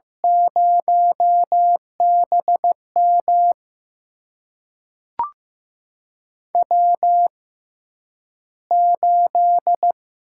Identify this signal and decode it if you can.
Morse code, heard as a keyed tone, about 15 wpm, spaced as for 9 wpm, 700 Hz. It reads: 0BM W8
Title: USA Call Signs - 6x Character Spacing - 15wpm